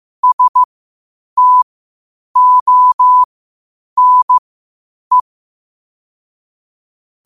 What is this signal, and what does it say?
Morse code, heard as a keyed tone, about 15 wpm, spaced as for 8 wpm, 1000 Hz. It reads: STONE
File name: qotdfiles / 8wpm